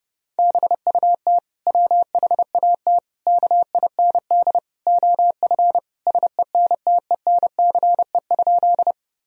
Morse code, heard as a keyed tone, about 30 wpm, 700 Hz. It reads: BUT WHAT KIND OF SENTENCE?
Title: Sentences from Top 1000 Words - Review Part 1 - 30wpm